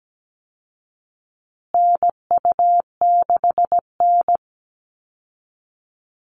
Morse code, heard as a keyed tone, about 17 wpm, 700 Hz. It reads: NU6N
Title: Call Signs Repeated 3x - 17wpm